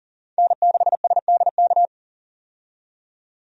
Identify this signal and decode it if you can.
Morse code, heard as a keyed tone, about 40 wpm, 700 Hz. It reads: N6SDK
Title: USA Call Signs - 5 Characters - 40wpm